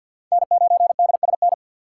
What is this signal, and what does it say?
Morse code, heard as a keyed tone, about 51 wpm, 700 Hz. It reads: N9DSN
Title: USA Call Signs - 5 Characters - 4x Character Spacing - 50wpm